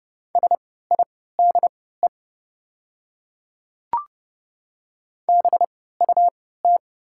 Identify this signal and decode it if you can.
Morse code, heard as a keyed tone, about 30 wpm, spaced as for 17 wpm, 700 Hz. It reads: SIDE BUT
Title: 3x Character Spacing - Top 100 Words - 30wpm